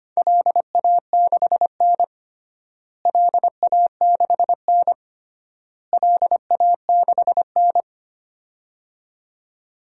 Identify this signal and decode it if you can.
Morse code, heard as a keyed tone, about 25 wpm, 700 Hz. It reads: LA6N LA6N LA6N
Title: DXCC-100 Call Signs - 4 Characters - Repeated 3x - 25wpm